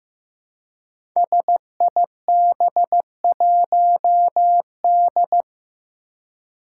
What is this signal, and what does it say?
Morse code, heard as a keyed tone, about 15 wpm, 700 Hz. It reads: SIB1D
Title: Five Challenging Letter-Number Combinations 15wpm